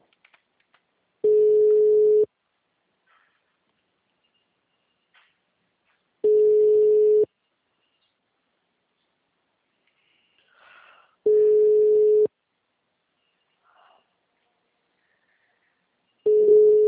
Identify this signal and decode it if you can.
Ringback tone (Europe, 425 Hz).
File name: out- -2015- - - .8869